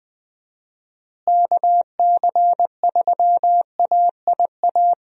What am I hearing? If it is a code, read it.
Morse code, heard as a keyed tone, about 20 wpm, 700 Hz. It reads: KC3AIA
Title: USA Call Signs - 6 Characters - Repeated 3x - 20wpm